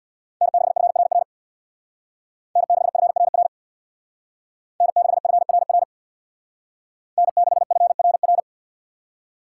Morse code, heard as a keyed tone, about 64 wpm, 700 Hz. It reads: N6FRR N6FRR N6FRR N6FRR